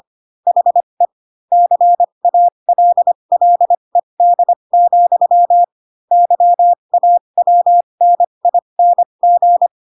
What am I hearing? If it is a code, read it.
Morse code, heard as a keyed tone, about 25 wpm, spaced as for 22 wpm, 700 Hz. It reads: HE CALLED, YAWNING
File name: Pinocchio / Pinocchio ch6 18 25 ik1hge